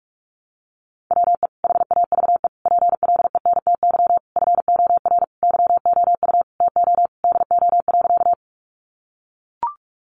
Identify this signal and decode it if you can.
Morse code, heard as a keyed tone, about 44 wpm, 700 Hz. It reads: WE HAVE PLENTY FOR YOU TO DO.
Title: Sentences from Top 1300 Words 45wpm